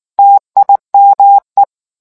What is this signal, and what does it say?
Morse code, heard as a keyed tone, about 19 wpm, 800 Hz. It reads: TIME